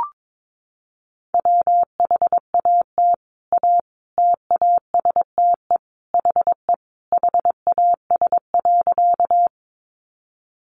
Morse code, heard as a keyed tone, about 22 wpm, 700 Hz. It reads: WHAT A TASTE HE HAS.
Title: Sentences from Top 1100 Words 22wpm